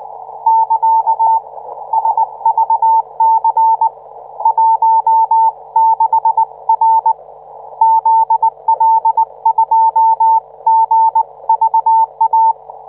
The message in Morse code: <KA> SVC 16R ZL2GVA